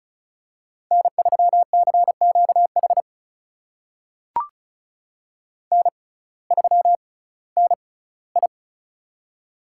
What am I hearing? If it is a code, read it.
Morse code, heard as a keyed tone, about 36 wpm, spaced as for 19 wpm, 700 Hz. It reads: N3CQH N3NI
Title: USA Call Signs - 6x Character Spacing - 35wpm